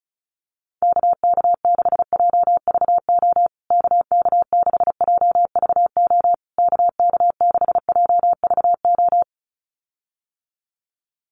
Morse code, heard as a keyed tone, about 34 wpm, 700 Hz. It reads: KK6JVO KK6JVO KK6JVO
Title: USA Call Signs Repeated 3x 35wpm